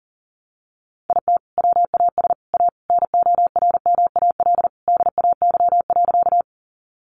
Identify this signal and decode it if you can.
Morse code, heard as a keyed tone, about 40 wpm, 700 Hz. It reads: IT WAS A NORMAL DAY.
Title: Sentences from Top 1200 Words 40wpm